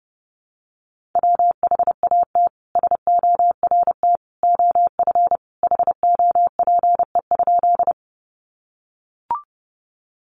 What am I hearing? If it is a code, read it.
Morse code, heard as a keyed tone, about 28 wpm, 700 Hz. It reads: WHAT SORT OF HOPE?
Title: Sentences from Top 1300 Words 30wpm